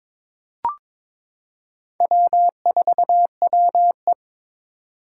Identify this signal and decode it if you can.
Morse code, heard as a keyed tone, about 22 wpm, 700 Hz. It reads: W4WE
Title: Four Challenging Letter-Number Combinations 22wpm